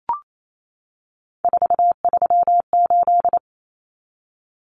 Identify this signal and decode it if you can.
Morse code, heard as a keyed tone, about 28 wpm, 700 Hz. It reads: 438
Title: Common Cut Numbers - 3-Digit 28wpm